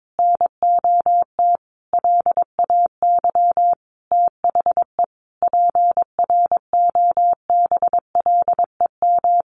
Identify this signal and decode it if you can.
Morse code, heard as a keyed tone, about 22 wpm, 700 Hz. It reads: NOT LAY THE PROBLEM
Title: Sentences from Top 1500 Words 22wpm